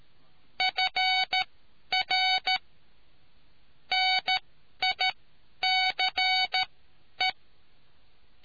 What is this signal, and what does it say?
Morse code, heard as a keyed tone, about 13 wpm, spaced as for 10 wpm, 750 Hz. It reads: FR NICE